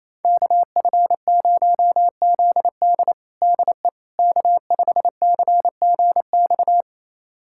Morse code, heard as a keyed tone, about 28 wpm, 690 Hz. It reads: KF0ZD DE K5CGX
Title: USA Callsign DE Callsign - 5 Characters - 28wpm